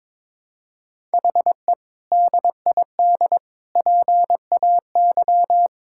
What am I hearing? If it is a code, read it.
Morse code, heard as a keyed tone, about 22 wpm, 700 Hz. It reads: HE DID PAY
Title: Sentences from Top 1000 Words - Review Part 6 - 22wpm